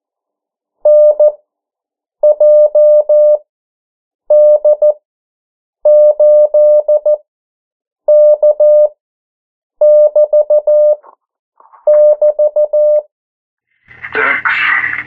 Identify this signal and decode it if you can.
Morse code, heard as a keyed tone, about 14 wpm, spaced as for 7 wpm, 600 Hz. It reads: NJD8K<BT><BT>
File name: morse mp3 / 09 urok 8